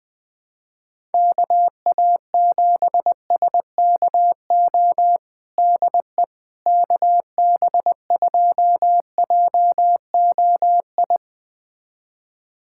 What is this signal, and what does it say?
Morse code, heard as a keyed tone, about 20 wpm, 700 Hz. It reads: KA7SKO DE KB2JOI
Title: USA Callsign DE Callsign - 6 Characters - 20wpm